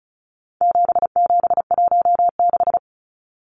Morse code, heard as a keyed tone, about 34 wpm, 700 Hz. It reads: 7716
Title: Numbers - 4-Digit 35wpm